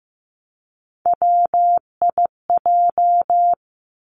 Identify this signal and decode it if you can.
Morse code, heard as a keyed tone, about 15 wpm, 700 Hz. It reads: WIJ
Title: Three Challenging Letter-Number Combinations 15wpm